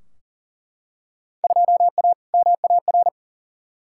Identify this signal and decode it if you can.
Morse code, heard as a keyed tone, about 40 wpm, 700 Hz. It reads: 2A MAR
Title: Field Day Exchanges - 40wpm